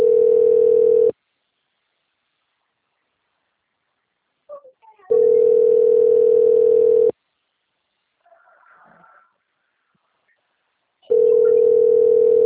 Ringback tone (North America), 440 + 480 Hz.
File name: rg-1600-unknown- - - .24413